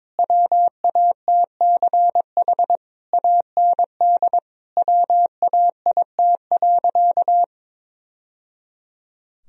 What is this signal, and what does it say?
Morse code, heard as a keyed tone, about 22 wpm, 700 Hz. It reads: WATCH AND WAIT.
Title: Sentences from Top 1000 Words - Review Part 4 - 22wpm